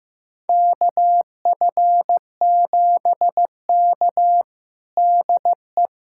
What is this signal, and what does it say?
Morse code, heard as a keyed tone, about 15 wpm, 700 Hz. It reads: KF7K DE
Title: USA Callsign DE Callsign - 4 Characters - 15wpm